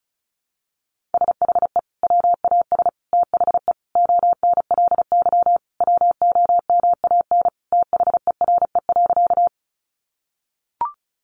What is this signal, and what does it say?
Morse code, heard as a keyed tone, about 34 wpm, 700 Hz. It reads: SHE WAS THE ONLY WOMAN THERE.
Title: Sentences from Top 1100 Words 35wpm